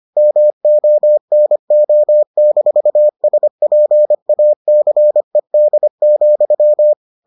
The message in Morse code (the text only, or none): MONO-SPACED,